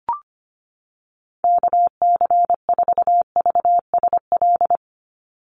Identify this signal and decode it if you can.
Morse code, heard as a keyed tone, about 25 wpm, 700 Hz. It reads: KC4VSL
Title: USA Call Signs 25wpm